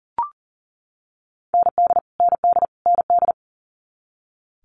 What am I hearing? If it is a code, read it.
Morse code, heard as a keyed tone, about 40 wpm, 700 Hz. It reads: ND ND ND
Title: Multi-letter Phonemes Repeated 3x 40wpm